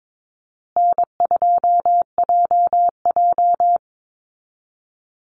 Morse code, heard as a keyed tone, about 22 wpm, 700 Hz. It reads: N2JJ
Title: Easy Call Signs Repeated 3x - 22wpm